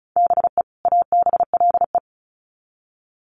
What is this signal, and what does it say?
Morse code, heard as a keyed tone, about 34 wpm, 700 Hz. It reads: BE ABLE
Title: Sets of 2 Words 35wpm